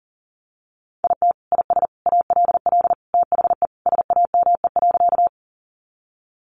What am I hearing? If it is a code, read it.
Morse code, heard as a keyed tone, about 40 wpm, 700 Hz. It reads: IT IS ALL THE SAME.
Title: Sentences from Top 200 Words 40wpm